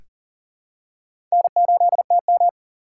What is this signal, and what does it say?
Morse code, heard as a keyed tone, about 40 wpm, 700 Hz. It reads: N8TM
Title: USA Call Signs - 4 Characters - 4x Character Spacing - 40wpm